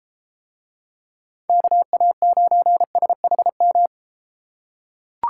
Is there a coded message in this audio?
Yes: Morse code from 1.5 to 3.9 s, heard as a keyed tone, about 32 wpm, 700 Hz. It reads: KA9SHM